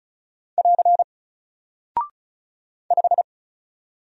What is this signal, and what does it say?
Morse code, heard as a keyed tone, about 36 wpm, 700 Hz. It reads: <AR> 5
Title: Single Character (Letter:Number:Punctuation:Prosign) Alt 35wpm